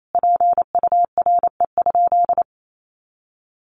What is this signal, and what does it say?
Morse code, heard as a keyed tone, about 28 wpm, 700 Hz. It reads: PURE?
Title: Sentences from Top 1500 Words 28wpm